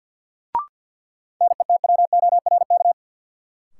Morse code, heard as a keyed tone, about 51 wpm, 700 Hz. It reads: NETWORK
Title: Sentences from Top 1000 Words - Review Part 7 - New Word Drill - 50wpm